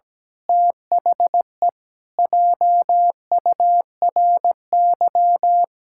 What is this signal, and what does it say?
Morse code, heard as a keyed tone, about 17 wpm, 700 Hz. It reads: THE JURY
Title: Pangram Challenge v2 - 17wpm